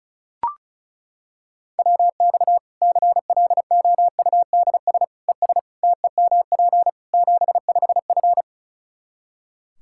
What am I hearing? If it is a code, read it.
Morse code, heard as a keyed tone, about 36 wpm, 700 Hz. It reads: WX CLOUDS ES TEMP 75F
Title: Practice QSO - Exchange 2 - Info - 35wpm